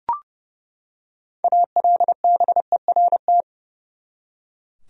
Morse code, heard as a keyed tone, about 30 wpm, 700 Hz. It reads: ALBERT
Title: USA Common Names - 30wpm